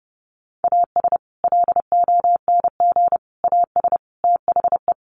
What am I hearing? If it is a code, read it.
Morse code, heard as a keyed tone, about 30 wpm, 700 Hz. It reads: AS LONG AS THE
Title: Sets of 4 Words 30wpm